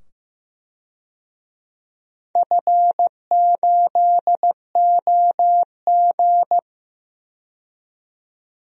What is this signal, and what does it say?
Morse code, heard as a keyed tone, about 15 wpm, 700 Hz. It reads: F8OG
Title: DXCC-100 Call Signs - 4 Characters - 15wpm